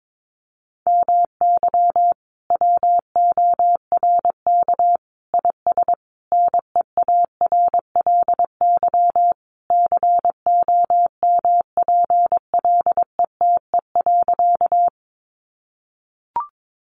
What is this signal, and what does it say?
Morse code, heard as a keyed tone, about 22 wpm, 700 Hz. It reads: MY WORK IS NEARLY COMPLETE.
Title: Sentences from Top 1300 Words 22wpm